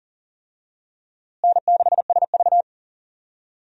Morse code, heard as a keyed tone, about 40 wpm, 700 Hz. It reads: N6SV
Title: USA Call Signs - 4 Characters - Repeated 3x - 40wpm